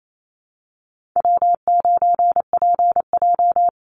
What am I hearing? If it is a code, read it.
Morse code, heard as a keyed tone, about 28 wpm, 700 Hz. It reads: W9PJ
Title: Call Signs Repeated 3x - 28wpm